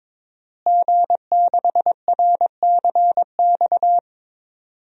Morse code, heard as a keyed tone, about 22 wpm, 700 Hz. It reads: G6RCX
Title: DXCC-100 Call Signs - 5 Characters - Repeated 3x - 22wpm